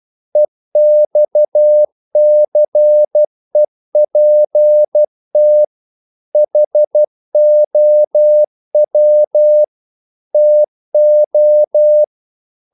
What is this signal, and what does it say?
Morse code, heard as a keyed tone, about 12 wpm, 600 Hz. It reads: EXCEPT HOW TO
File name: CW / Practice / EnglishQuotations0312